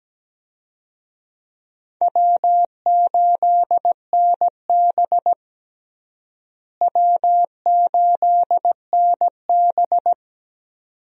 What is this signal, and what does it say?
Morse code, heard as a keyed tone, about 17 wpm, 700 Hz. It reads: W8NB W8NB
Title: USA Call Signs - 4 Characters - Repeated 2x - 17wpm